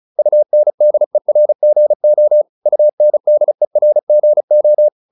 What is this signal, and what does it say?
Morse code, heard as a keyed tone, about 34 wpm, 600 Hz. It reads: UNDERGO UNDERGO